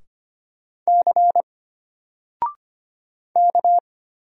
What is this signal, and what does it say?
Morse code, heard as a keyed tone, about 25 wpm, 700 Hz. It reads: C K
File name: Single-Letter-Number-25wpm